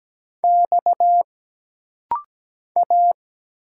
Morse code, heard as a keyed tone, about 17 wpm, 700 Hz. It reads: X A
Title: Single Letters - Rapid-Fire 17wpm